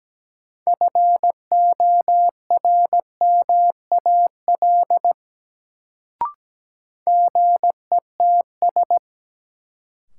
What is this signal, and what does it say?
Morse code, heard as a keyed tone, about 17 wpm, 700 Hz. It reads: FORMAL GETS
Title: Sentences from Top 1300 Words - New Word Drill Alt - 17wpm